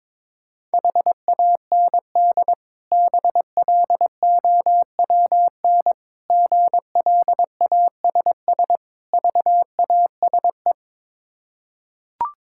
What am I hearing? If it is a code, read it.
Morse code, heard as a keyed tone, about 22 wpm, 690 Hz. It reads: HAND BLOWN GLASS VASE
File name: Difficult Words with Hints - 22wpm